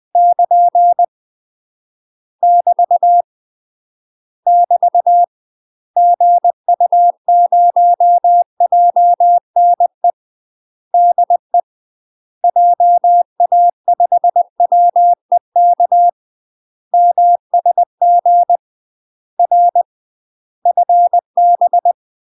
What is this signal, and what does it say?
Morse code, heard as a keyed tone, about 20 wpm, 700 Hz. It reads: <KN> <BT> <BT> GU0JNE DE JA5WEK MSG R FB
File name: UK MASTER QSOs TRAINING / 20WPM / UK MASTER QSOs 0060